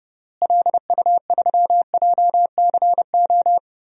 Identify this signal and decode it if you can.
Morse code, heard as a keyed tone, about 30 wpm, 700 Hz. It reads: LU3JCO
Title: DXCC-100 Call Signs - 6 Characters - Repeated 3x - 30wpm